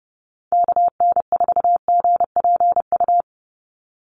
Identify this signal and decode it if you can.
Morse code, heard as a keyed tone, about 29 wpm, 700 Hz. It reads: KN4GPU
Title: USA Call Signs 30wpm